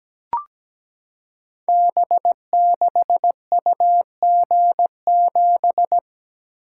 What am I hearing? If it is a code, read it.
Morse code, heard as a keyed tone, about 17 wpm, 700 Hz. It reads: B6UG7
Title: Code Groups - 5 Letter:Numbers - 17wpm